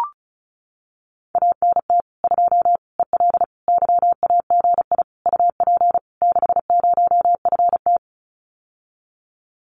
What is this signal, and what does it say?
Morse code, heard as a keyed tone, about 34 wpm, 700 Hz. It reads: ANT 2 EL YAGI UP 60FT
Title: Common QSO Phrases Encore 35wpm